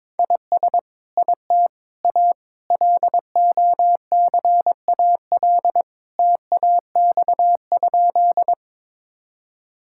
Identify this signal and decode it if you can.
Morse code, heard as a keyed tone, about 22 wpm, 700 Hz. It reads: IS IT A LOCAL TAX?
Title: Sentences from Top 1000 Words - Review Part 5 - 22wpm